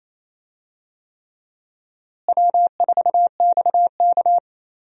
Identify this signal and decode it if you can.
Morse code, heard as a keyed tone, about 28 wpm, 700 Hz. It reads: W4XK
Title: USA Call Signs - 4 Characters - Repeated 3x - 28wpm